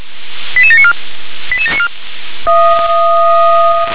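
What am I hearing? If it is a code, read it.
Five-tone selective call (ZVEI), heard as quick runs of five tones, tones of about 70 ms. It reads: 88794 88794
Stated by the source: ZVEI five-tone call received by SDR.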